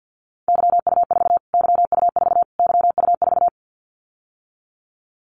Morse code, heard as a keyed tone, about 50 wpm, 700 Hz. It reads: YU4 YU4 YU4
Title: Callsign Prefixes for Top 25 DXCC - Repeated 3x - 50wpm